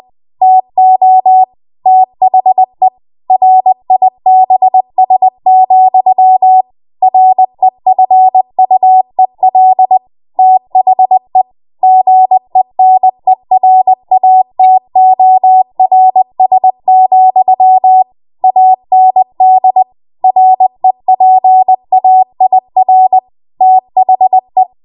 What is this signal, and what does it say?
Morse code, heard as a keyed tone, about 20 wpm, 750 Hz. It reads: TO THE RIBS, REFUEL THE GENERATORS, AND REPAIR THE
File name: Morse / 20WPM